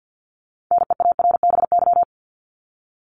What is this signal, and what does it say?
Morse code, heard as a keyed tone, about 50 wpm, 700 Hz. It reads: NEARBY
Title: Top 1501-1750 Words 50wpm